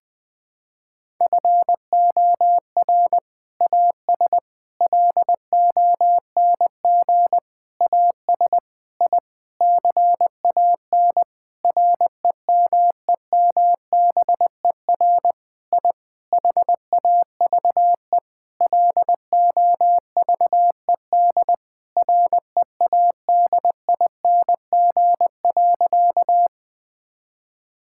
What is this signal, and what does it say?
Morse code, heard as a keyed tone, about 20 wpm, 700 Hz. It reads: FOR AS LONG AS I CAN REMEMBER I HAVE LOVED READING.
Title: Sentences with 8-Word Phrases - 20wpm